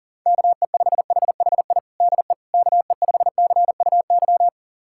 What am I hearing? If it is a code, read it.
Morse code, heard as a keyed tone, about 40 wpm, 700 Hz. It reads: KE5HHI DE KE5CUY
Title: USA Callsign DE Callsign - 6 Characters - 40wpm